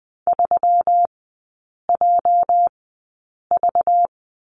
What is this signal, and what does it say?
Morse code, heard as a keyed tone, about 20 wpm, spaced as for 8 wpm, 690 Hz. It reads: 3JV
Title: 212 Lesson 34e - Number 8 - Three Characters 20wpm